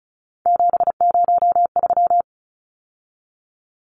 Morse code, heard as a keyed tone, about 34 wpm, 700 Hz. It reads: 703
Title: Numbers - 3-Digit 35wpm